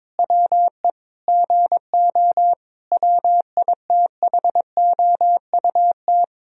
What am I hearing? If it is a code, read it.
Morse code, heard as a keyed tone, about 22 wpm, 700 Hz. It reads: WE GO WITHOUT
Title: Sentences from Top 1000 Words - Review Part 3 - 22wpm